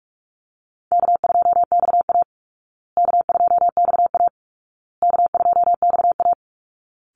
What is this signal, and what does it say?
Morse code, heard as a keyed tone, about 46 wpm, 700 Hz. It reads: K2XA K2XA K2XA
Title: Easy Call Signs Repeated 3x - 45wpm